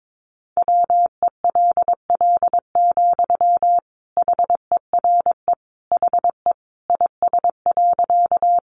Morse code, heard as a keyed tone, about 22 wpm, 700 Hz. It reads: WELL, HERE HE IS.
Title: Sentences from Top 200 Words 22wpm